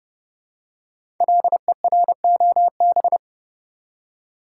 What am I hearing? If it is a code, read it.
Morse code, heard as a keyed tone, about 30 wpm, 700 Hz. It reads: LEROB
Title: Code Groups - 5 Letters - 30wpm